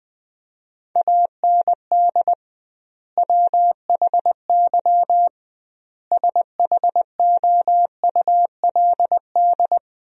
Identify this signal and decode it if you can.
Morse code, heard as a keyed tone, about 20 wpm, 700 Hz. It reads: AND WHY SHOULD
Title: 2x Word-Spacing - Sentences from Top 300 Words - 20wpm